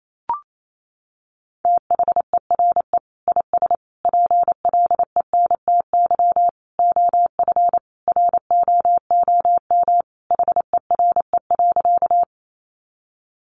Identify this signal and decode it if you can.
Morse code, heard as a keyed tone, about 28 wpm, 700 Hz. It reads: THERE IS PLENTY OF ROOM HERE.
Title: Sentences from Top 1300 Words 28wpm